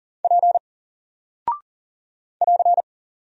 Morse code, heard as a keyed tone, about 40 wpm, 700 Hz. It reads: P <AR>
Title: Single Character (Letter:Number:Punctuation:Prosign) 40wpm